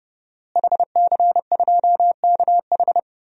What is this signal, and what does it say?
Morse code, heard as a keyed tone, about 32 wpm, 700 Hz. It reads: HC2KH
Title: DXCC-100 Call Signs - 5 Characters - Repeated 2x - 30wpm